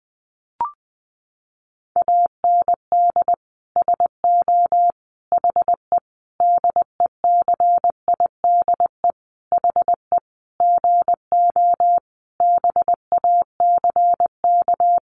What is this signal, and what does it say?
Morse code, heard as a keyed tone, about 20 wpm, 700 Hz. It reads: AND SO HE DECIDE HE GO BACK
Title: Sentences from Top 500 Words 20wpm